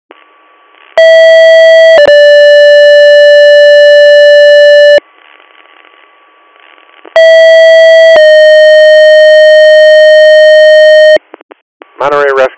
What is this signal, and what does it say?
Two-tone page, heard as a one-second tone, then a three-second tone, about 650 then 580 Hz, about 650 then 620 Hz, with speech. It is heard from 1.0 to 11.2 s.